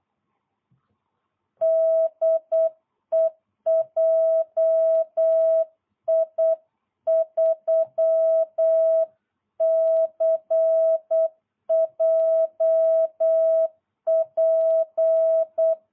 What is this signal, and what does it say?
Morse code, heard as a keyed tone, about 8 wpm, 660 Hz. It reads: DE JI3CJP